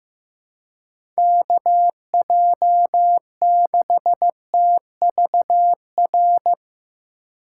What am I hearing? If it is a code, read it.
Morse code, heard as a keyed tone, about 15 wpm, 700 Hz. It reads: KJ6TVR
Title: USA Call Signs - 6 Characters - 2x Character Spacing - 15wpm